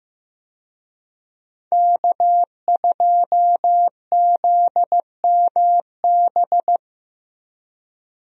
Morse code, heard as a keyed tone, about 15 wpm, 700 Hz. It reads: K2ZMB
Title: USA Call Signs - 5 Characters - Repeated 3x - 15wpm